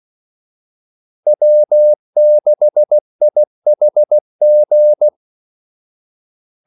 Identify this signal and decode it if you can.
Morse code, heard as a keyed tone, about 16 wpm, 600 Hz. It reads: W6IHG
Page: Kaldesignaler 16 wpm